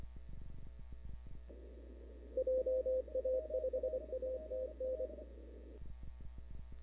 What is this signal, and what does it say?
Morse code, heard as a keyed tone, about 24 wpm, 500 Hz. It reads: JA5ATN
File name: CW / 2005-k5tr arrl cw dx -00:00